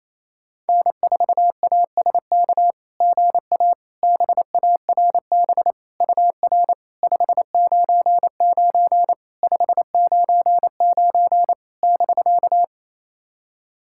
Morse code, heard as a keyed tone, about 28 wpm, 700 Hz. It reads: N4ASK GA BARB UR 599 599 <BK>
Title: SOTA QSO Practice 28wpm